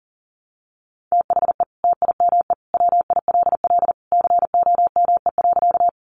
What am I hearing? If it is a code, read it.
Morse code, heard as a keyed tone, about 40 wpm, 700 Hz. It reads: THE TIME WILL COME.
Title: Sentences from Top 100 Words 40wpm